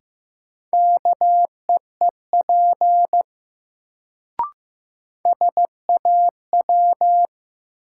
Morse code, heard as a keyed tone, about 15 wpm, 700 Hz. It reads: KEEP SAW
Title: Sentences from Top 300 Words - New Word Drill - 15wpm